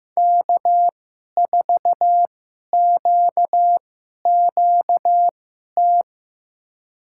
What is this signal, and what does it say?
Morse code, heard as a keyed tone, about 15 wpm, spaced as for 11 wpm, 700 Hz. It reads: K4QQT
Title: USA Call Signs - 5 Characters - 2x Character Spacing - 15wpm